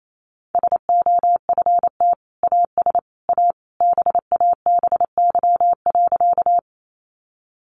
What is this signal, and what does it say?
Morse code, heard as a keyed tone, about 28 wpm, 700 Hz. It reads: SOFT AS A BABY.
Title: Sentences from Top 1500 Words 28wpm